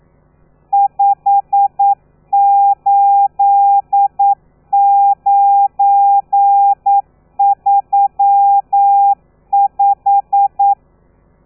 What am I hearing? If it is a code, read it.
Morse code, heard as a keyed tone, about 9 wpm, 800 Hz. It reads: 58935
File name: telegrafia / i2tfj / Lez12 40